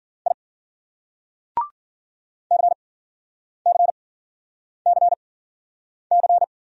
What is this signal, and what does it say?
Morse code, heard as a keyed tone, about 50 wpm, spaced as for 9 wpm, 700 Hz. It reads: I CCCC